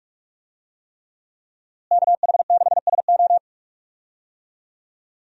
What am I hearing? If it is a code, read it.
Morse code, heard as a keyed tone, about 46 wpm, 700 Hz. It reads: KH6SO
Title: USA Call Signs - 5 Characters - Repeated 3x - 45wpm